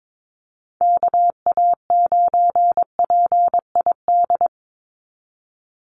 Morse code, heard as a keyed tone, about 22 wpm, 700 Hz. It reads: KA9PID
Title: USA Call Signs 22wpm